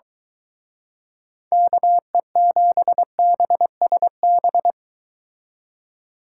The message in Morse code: KE7BSB

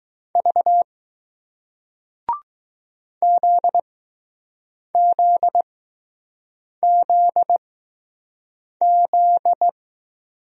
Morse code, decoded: V ZZZZ

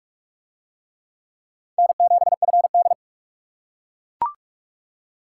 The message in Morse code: N7FD

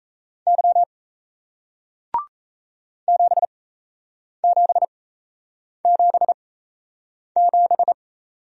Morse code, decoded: Y 7777

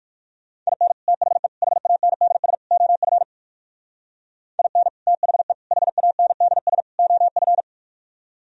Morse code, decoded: IN THE HANDS OF IN THE HANDS OF

